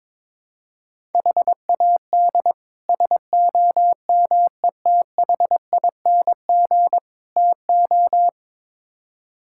Morse code, heard as a keyed tone, about 22 wpm, 700 Hz. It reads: HAD SOMETHING TO
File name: practice / 22wpm / Build A Phrase - 5-Word Phrases - 22wpm